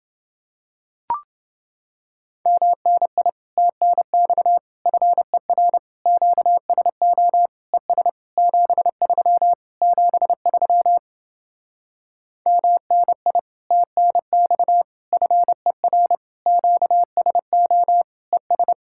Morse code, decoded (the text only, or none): MNI TNX FER QSO ES 73 73 MNI TNX FER QSO ES